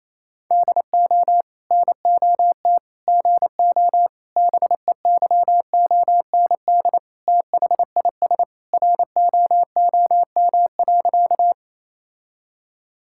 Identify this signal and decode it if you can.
Morse code, heard as a keyed tone, about 28 wpm, 700 Hz. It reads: DO NOT GO BEYOND THIS ROOM.
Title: Sentences from Top 1000 Words - Review Part 6 - 28wpm